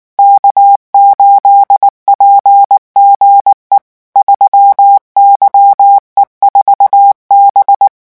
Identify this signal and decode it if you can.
Morse code, heard as a keyed tone, about 19 wpm, 800 Hz. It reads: K8PGE 3YE4B
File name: morse mp3 / revision12-vitesse 18 mots